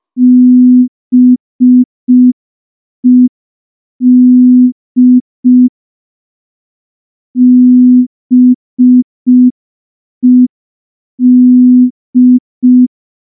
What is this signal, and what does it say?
Morse code, heard as a keyed tone, about 5 wpm, 250 Hz. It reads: BED BED
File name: morse / morse-training-05WPM